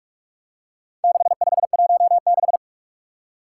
Morse code, heard as a keyed tone, about 44 wpm, 700 Hz. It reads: 6516
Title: Four Challenging Letter-Number Combinations 45wpm